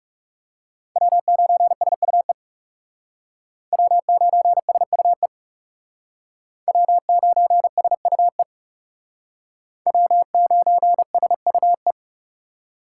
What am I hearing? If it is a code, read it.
Morse code, heard as a keyed tone, about 37 wpm, 700 Hz. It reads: W9SUE W9SUE W9SUE W9SUE
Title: Speed Racing - USA 1x3 Call Signs 30wpm